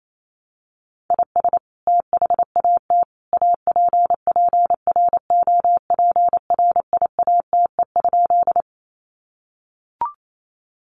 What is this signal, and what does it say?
Morse code, heard as a keyed tone, about 28 wpm, 700 Hz. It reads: IS THAT APPROPRIATE?
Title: Sentences from Top 900 Words 28wpm